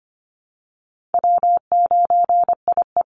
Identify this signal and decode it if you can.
Morse code, heard as a keyed tone, about 25 wpm, 700 Hz. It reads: W9IE
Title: Easy Call Signs Repeated 2x - 25wpm